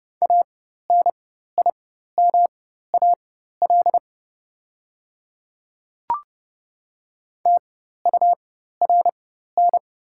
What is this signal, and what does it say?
Morse code, heard as a keyed tone, about 30 wpm, spaced as for 14 wpm, 700 Hz. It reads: ANIMAL TURN
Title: 4x Character Spacing - Top 500 Words - 30wpm